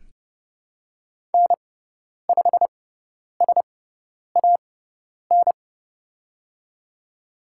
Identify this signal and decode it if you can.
Morse code, heard as a keyed tone, about 29 wpm, spaced as for 10 wpm, 700 Hz. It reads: N5SAN
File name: Lesson 17g - Letter u - Call Signs 30wpm